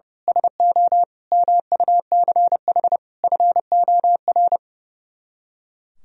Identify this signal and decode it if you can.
Morse code, heard as a keyed tone, about 28 wpm, 700 Hz. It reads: SO MUCH FOR
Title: Speed-Racing - Sets of 5 Words 30wpm